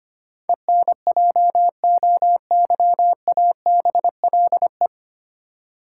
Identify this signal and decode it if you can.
Morse code, heard as a keyed tone, about 25 wpm, 700 Hz. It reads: ENJOYABLE
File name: practice / 25wpm / 9 Letter Words - First 3 Letters Given - 25wpm